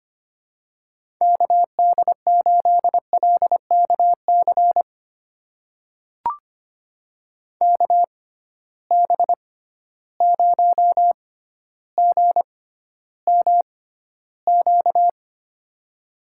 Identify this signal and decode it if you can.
Morse code, heard as a keyed tone, about 25 wpm, spaced as for 13 wpm, 700 Hz. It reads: KD8LKC KB0GMQ
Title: USA Call Signs - 6 Characters - 6x Character Spacing - 25wpm